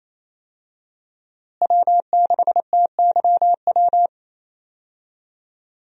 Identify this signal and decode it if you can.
Morse code, heard as a keyed tone, about 28 wpm, 700 Hz. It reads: W6TYW
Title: USA Call Signs - 5 Characters - Repeated 2x - 28wpm